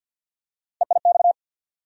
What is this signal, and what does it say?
Morse code, heard as a keyed tone, about 52 wpm, 700 Hz. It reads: EIX
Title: USA Call Sign Postfixes - 3 Letters 50wpm